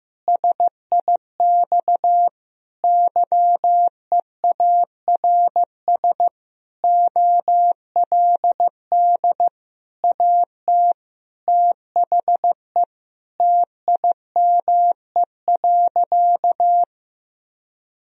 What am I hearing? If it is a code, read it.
Morse code, heard as a keyed tone, about 15 wpm, 700 Hz. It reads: SIX YEARS OLD AT THE TIME.
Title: Sentences from Top 1000 Words - Review Part 2 - 15wpm